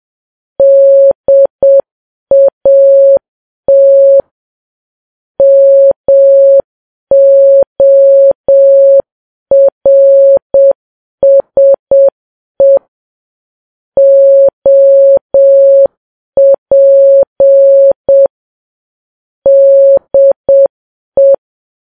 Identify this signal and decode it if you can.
Morse code, heard as a keyed tone, about 7 wpm, 550 Hz. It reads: DAT MORSE OP DE